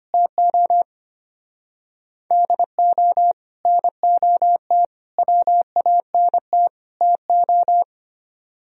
Morse code, heard as a keyed tone, about 31 wpm, 700 Hz. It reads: TO DO NOT WANT TO